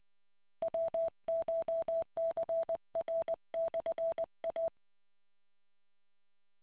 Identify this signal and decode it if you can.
Morse code, heard as a keyed tone, about 20 wpm, 660 Hz. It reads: WCHCR/A